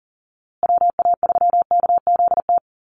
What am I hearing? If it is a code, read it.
Morse code, heard as a keyed tone, about 39 wpm, 700 Hz. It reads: WA3KZT
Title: USA Call Signs Repeated 3x 40wpm